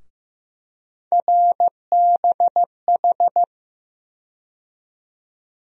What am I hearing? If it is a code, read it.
Morse code, heard as a keyed tone, about 15 wpm, 700 Hz. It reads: RBH
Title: USA Call Sign Postfixes - 3 Letters 15wpm